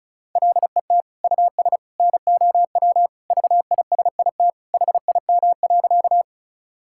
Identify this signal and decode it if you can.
Morse code, heard as a keyed tone, about 34 wpm, 700 Hz. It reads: LET US NOW VISIT HIM.
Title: Sentences from Top 1000 Words - Review Part 6 - 35wpm